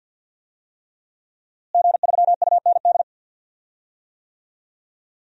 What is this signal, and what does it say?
Morse code, heard as a keyed tone, about 52 wpm, 700 Hz. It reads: G3UND
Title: DXCC-100 Call Signs - 5 Characters - Repeated 3x - 50wpm